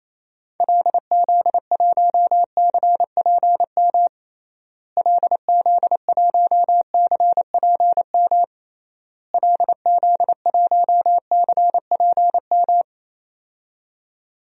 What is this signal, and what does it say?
Morse code, heard as a keyed tone, about 28 wpm, 700 Hz. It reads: LZ1CPM LZ1CPM LZ1CPM
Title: DXCC-100 Call Signs - 6 Characters - Repeated 3x - 28wpm